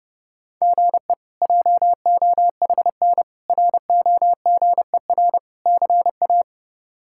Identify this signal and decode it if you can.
Morse code, heard as a keyed tone, about 30 wpm, 700 Hz. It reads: GE JOHN ROGER CA
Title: K1USN SST QSO Practice - 30wpm